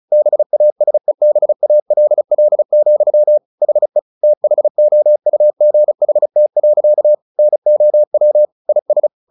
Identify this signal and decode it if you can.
Morse code, heard as a keyed tone, about 34 wpm, 600 Hz. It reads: BASEBALL, HE THOUGHT. NOW IS
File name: the old man and the sea / the old man and the sea 1 35wpm